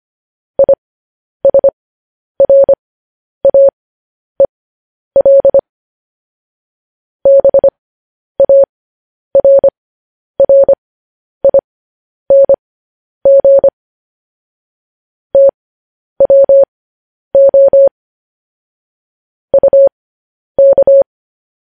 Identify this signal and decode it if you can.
Morse code, heard as a keyed tone, about 25 wpm, spaced as for 10 wpm, 550 Hz. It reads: ISRAEL BARRING TWO UK
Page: news headlines in morse code at 10 wpm